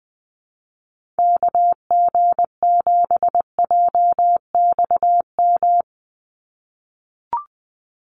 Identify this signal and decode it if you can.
Morse code, heard as a keyed tone, about 20 wpm, 700 Hz. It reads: KG7JXM
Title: USA Call Signs 20wpm